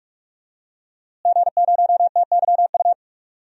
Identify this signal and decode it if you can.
Morse code, heard as a keyed tone, about 46 wpm, 700 Hz. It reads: G0TYU
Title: DXCC-100 Call Signs - 5 Characters - Repeated 2x - 45wpm